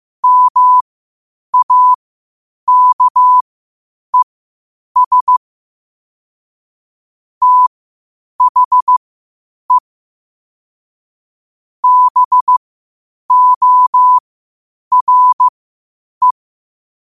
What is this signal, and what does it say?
Morse code, heard as a keyed tone, about 15 wpm, spaced as for 8 wpm, 1000 Hz. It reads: MAKES THE BORE
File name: qotdfiles / 8wpm